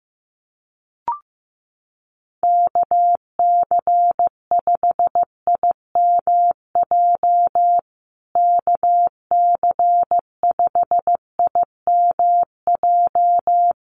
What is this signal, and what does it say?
Morse code, heard as a keyed tone, about 15 wpm, 700 Hz. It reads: KC5IMJ KC5IMJ
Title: USA Call Signs Repeated 3x 15wpm